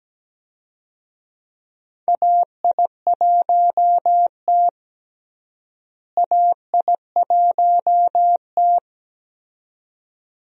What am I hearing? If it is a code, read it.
Morse code, heard as a keyed tone, about 17 wpm, 700 Hz. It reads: AI1T AI1T
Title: USA Call Signs - 4 Characters - Repeated 2x - 17wpm